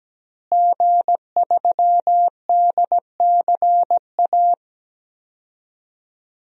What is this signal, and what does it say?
Morse code, heard as a keyed tone, about 17 wpm, 700 Hz. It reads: G3DCA
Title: DXCC-100 Call Signs - 5 Characters - Repeated 3x - 17wpm